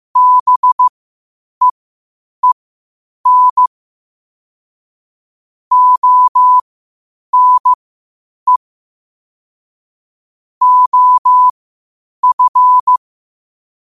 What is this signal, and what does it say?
Morse code, heard as a keyed tone, about 15 wpm, spaced as for 8 wpm, 1000 Hz. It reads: BEEN ONE OF
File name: qotdfiles / 8wpm